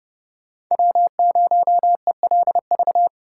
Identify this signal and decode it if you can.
Morse code, heard as a keyed tone, about 30 wpm, 700 Hz. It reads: W0ELV
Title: Code Groups - 5 Letter:Numbers - 30wpm